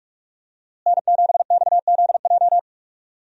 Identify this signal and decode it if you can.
Morse code, heard as a keyed tone, about 46 wpm, 700 Hz. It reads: N7XZJ
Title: USA Call Signs - 5 Characters - Repeated 3x - 45wpm